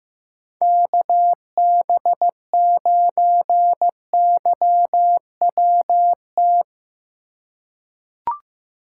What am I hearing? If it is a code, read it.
Morse code, heard as a keyed tone, about 15 wpm, 700 Hz. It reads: KB9YWT
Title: USA Call Signs - 6 Characters - 3x Character Spacing - 15wpm